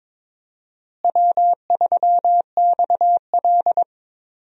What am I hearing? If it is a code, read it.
Morse code, heard as a keyed tone, about 22 wpm, 700 Hz. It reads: W3XL
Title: USA Call Signs - 4 Characters - Repeated 3x - 22wpm